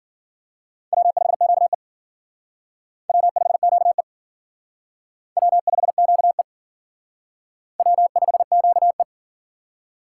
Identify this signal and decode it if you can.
Morse code, heard as a keyed tone, about 50 wpm, 700 Hz. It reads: W5QE W5QE W5QE W5QE
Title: Speed Racing - USA 1x2 Call Signs 40wpm